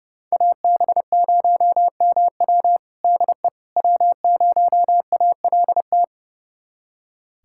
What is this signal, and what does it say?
Morse code, heard as a keyed tone, about 30 wpm, 700 Hz. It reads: AB0MW DE W0ALT
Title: USA Callsign DE Callsign - 5 Characters - 30wpm